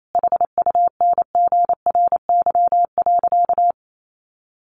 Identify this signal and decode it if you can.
Morse code, heard as a keyed tone, about 28 wpm, 700 Hz. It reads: HUNGRY.
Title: Sentences from Top 1400 Words 28wpm